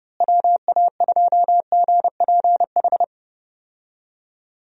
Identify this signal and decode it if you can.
Morse code, heard as a keyed tone, about 32 wpm, 700 Hz. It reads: WA2GPH
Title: USA Call Signs - 6 Characters - Repeated 3x - 30wpm